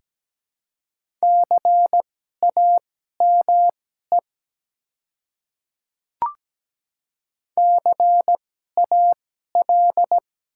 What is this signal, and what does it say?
Morse code, heard as a keyed tone, about 17 wpm, spaced as for 12 wpm, 700 Hz. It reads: CAME CAL
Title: 2x Character Spacing - Top 300 Words - 17wpm